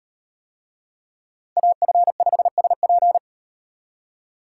Morse code, heard as a keyed tone, about 36 wpm, 700 Hz. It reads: AF5SP